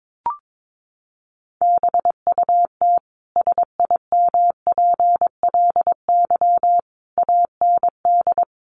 Morse code, heard as a keyed tone, about 22 wpm, 700 Hz. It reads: BUT SIMPLY AND
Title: Sentences from Top 1300 Words 22wpm